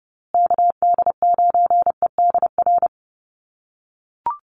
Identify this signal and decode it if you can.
Morse code, heard as a keyed tone, about 30 wpm, 700 Hz. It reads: KD9EDR
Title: USA Call Signs Repeated 2x 30wpm